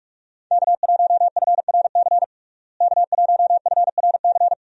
Morse code, heard as a keyed tone, about 47 wpm, 700 Hz. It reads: K1FRC K1FRC
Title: USA Call Signs - 5 Characters - Repeated 2x - 45wpm